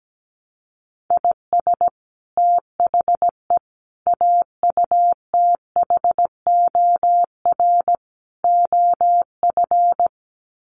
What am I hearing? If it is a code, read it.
Morse code, heard as a keyed tone, about 17 wpm, 700 Hz. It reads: IS THE AUTHOR OF
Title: Sets of 4 Words 17wpm